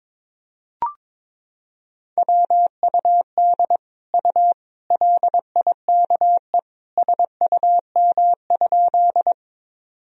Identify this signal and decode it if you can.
Morse code, heard as a keyed tone, about 22 wpm, 700 Hz. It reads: WUD U LIKE SUM?
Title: Sentences from Top 100 Words and Abbreviations 22wpm